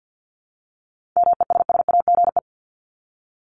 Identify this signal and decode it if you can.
Morse code, heard as a keyed tone, about 50 wpm, 700 Hz. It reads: MESSAGE
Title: Top 1001-1250 Words 50wpm